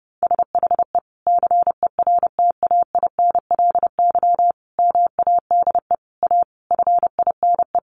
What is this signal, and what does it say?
Morse code, heard as a keyed tone, about 30 wpm, 700 Hz. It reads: SHE CERTAINLY MADE A FINE
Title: Sentences from Top 1100 Words 30wpm